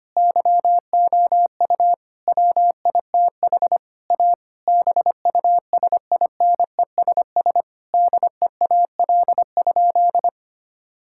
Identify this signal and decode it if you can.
Morse code, heard as a keyed tone, about 25 wpm, 700 Hz. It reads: YOU WITH A BUSINESS DEAL?
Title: Sentences from Top 1000 Words - Review Part 6 - 25wpm